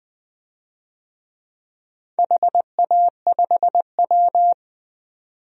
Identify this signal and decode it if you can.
Morse code, heard as a keyed tone, about 20 wpm, 700 Hz. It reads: HA5W
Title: DXCC-100 Call Signs - 4 Characters - 20wpm